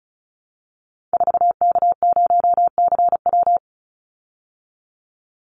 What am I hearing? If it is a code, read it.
Morse code, heard as a keyed tone, about 34 wpm, 700 Hz. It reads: 4K0CW
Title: Call Signs Repeated 3x - 35wpm